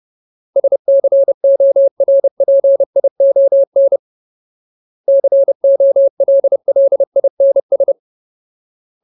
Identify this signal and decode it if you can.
Morse code, heard as a keyed tone, about 30 wpm, 550 Hz. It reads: SCORPION COLLINS
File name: Ultimate Word List / 30WPM Ultimate